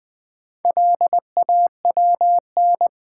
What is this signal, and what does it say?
Morse code, heard as a keyed tone, about 20 wpm, 700 Hz. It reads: LAWN
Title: Build A Word - 1 Letter At A Time - 3 to 5-Letter Words - 20wpm